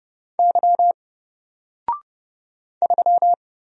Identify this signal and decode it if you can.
Morse code, heard as a keyed tone, about 30 wpm, 700 Hz. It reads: Y 3
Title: Single Character (Letter:Number:Punctuation:Prosign) Alt 30wpm